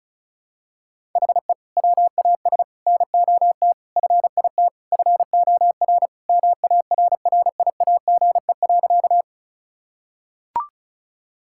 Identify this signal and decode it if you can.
Morse code, heard as a keyed tone, about 36 wpm, 700 Hz. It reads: HE WAS NOT FIT FOR MARRIAGE.
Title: Sentences from Top 1000 Words - Review Part 5 - 35wpm